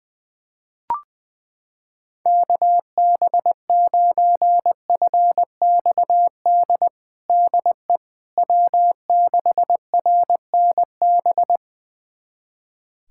Morse code, heard as a keyed tone, about 21 wpm, 700 Hz. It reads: KB9FXD DE W6RNB
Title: USA Callsign DE Callsign 20wpm